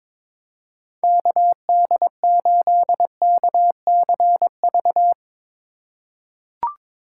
Morse code, heard as a keyed tone, about 22 wpm, 700 Hz. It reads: KD8KCV
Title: USA Call Signs - 4x Character Spacing - 22wpm